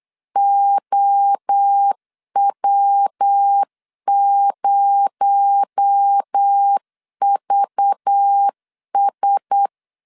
Morse code, heard as a keyed tone, about 8 wpm, 800 Hz. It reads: OW0VS
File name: LEZIONI / 6 PKX0 / 6 PKX0 8wpm 3341